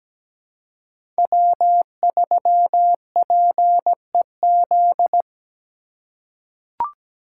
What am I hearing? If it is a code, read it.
Morse code, heard as a keyed tone, about 17 wpm, 700 Hz. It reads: W3PEZ
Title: USA Call Signs - 3x Character Spacing - 17wpm